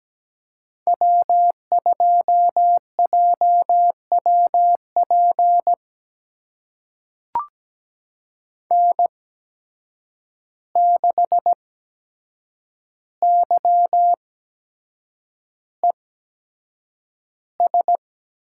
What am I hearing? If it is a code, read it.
Morse code, heard as a keyed tone, about 17 wpm, spaced as for 7 wpm, 700 Hz. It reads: W2JWP N6YES
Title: USA Call Signs - 5 Characters - 8x Character Spacing - 17wpm